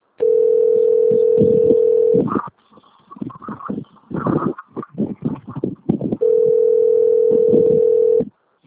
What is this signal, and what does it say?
Ringback tone (North America), 440 + 480 Hz.
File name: rg-1600-unknown- - - .39788